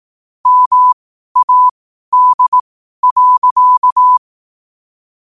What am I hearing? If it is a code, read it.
Morse code, heard as a keyed tone, about 18 wpm, spaced as for 12 wpm, 1000 Hz. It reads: MAD.
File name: qotdfiles / 12wpm